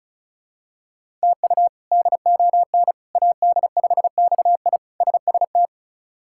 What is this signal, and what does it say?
Morse code, heard as a keyed tone, about 36 wpm, 700 Hz. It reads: TU DON AD5XI SST
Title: K1USN SST QSO Practice - 35wpm